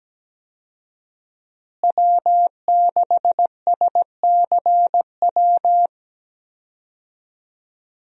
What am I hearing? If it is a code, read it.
Morse code, heard as a keyed tone, about 17 wpm, 700 Hz. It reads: W6SCW
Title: USA Call Signs - 5 Characters - Repeated 2x - 17wpm